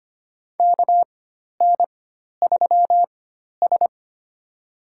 Morse code, heard as a keyed tone, about 25 wpm, spaced as for 12 wpm, 700 Hz. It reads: KN3S
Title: USA Call Signs - 4 Characters - 4x Character Spacing - 25wpm